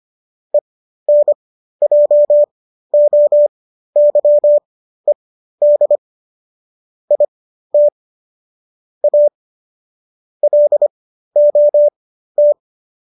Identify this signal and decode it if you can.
Morse code, heard as a keyed tone, about 25 wpm, spaced as for 13 wpm, 590 Hz. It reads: ENJOYED IT A LOT